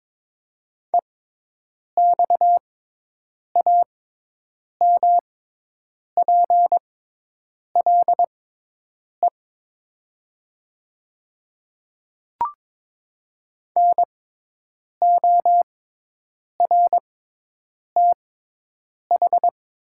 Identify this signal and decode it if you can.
Morse code, heard as a keyed tone, about 22 wpm, spaced as for 8 wpm, 700 Hz. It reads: EXAMPLE NORTH
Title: 6x Character Spacing - Top 500 Words - 22wpm